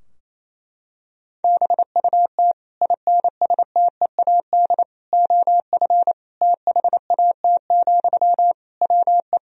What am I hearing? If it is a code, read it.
Morse code, heard as a keyed tone, about 28 wpm, 700 Hz. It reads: BUT INSTEAD OF THAT, WE
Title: Sentences from Top 1000 Words - Review Part 4 - 28wpm